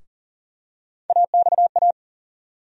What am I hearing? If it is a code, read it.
Morse code, heard as a keyed tone, about 40 wpm, 700 Hz. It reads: AXA
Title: USA Call Sign Postfixes - 3 Letters 40wpm